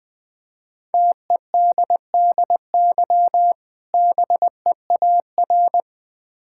Morse code, heard as a keyed tone, about 21 wpm, 700 Hz. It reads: TEDDY BEAR
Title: Difficult Words with Hints - 20wpm